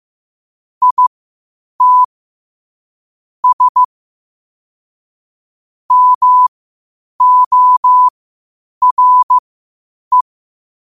Morse code, heard as a keyed tone, about 15 wpm, spaced as for 8 wpm, 1000 Hz. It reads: IT S MORE
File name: qotdfiles / 8wpm